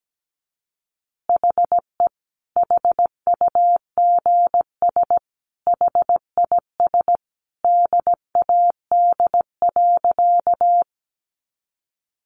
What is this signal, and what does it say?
Morse code, heard as a keyed tone, about 17 wpm, 700 Hz. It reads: HE HUGS HIS DAD.
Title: Easy Phrases 17wpm